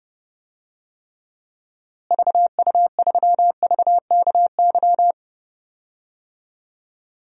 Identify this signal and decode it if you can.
Morse code, heard as a keyed tone, about 32 wpm, 700 Hz. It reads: VU3VKY
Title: DXCC-100 Call Signs - 6 Characters - Repeated 3x - 30wpm